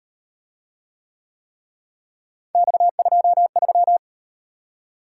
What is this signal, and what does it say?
Morse code, heard as a keyed tone, about 38 wpm, 700 Hz. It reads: X23